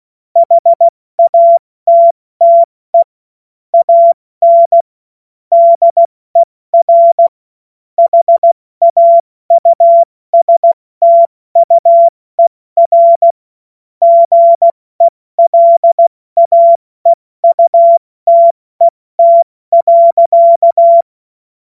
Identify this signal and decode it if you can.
Morse code, heard as a keyed tone, about 16 wpm, spaced as for 14 wpm, 680 Hz. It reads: HATTE AN DER HAUSTUER GELAEUTET.
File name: Morsekurs / Witze / Witz 09 Januar 2024 70 F80